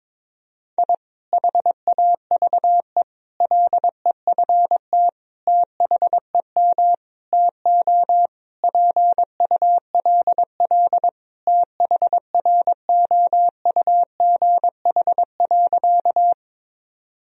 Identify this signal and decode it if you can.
Morse code, heard as a keyed tone, about 22 wpm, 700 Hz. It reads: I HAVE LEFT THEM TO PULL THROUGH.
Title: Sentences from Top 1000 Words - Review Part 2 - 22wpm